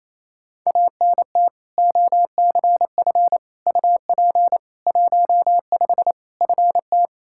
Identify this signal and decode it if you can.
Morse code, heard as a keyed tone, about 28 wpm, 700 Hz. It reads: ANT OCF UP 15 FT
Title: Practice QSO - Exchange 2 - Info - 28wpm